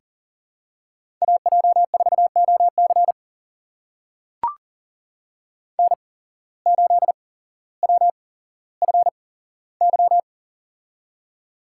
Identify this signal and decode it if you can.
Morse code, heard as a keyed tone, about 40 wpm, spaced as for 17 wpm, 700 Hz. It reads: AJ4OC N8WFY
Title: USA Call Signs - 5 Characters - 8x Character Spacing - 40wpm